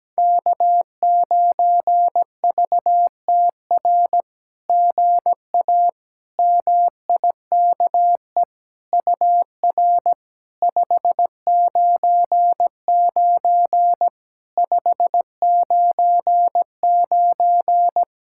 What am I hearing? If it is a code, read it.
Morse code, heard as a keyed tone, about 17 wpm, 700 Hz. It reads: K9VTR GA MIKE UR 599 599
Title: SOTA QSO Practice 17wpm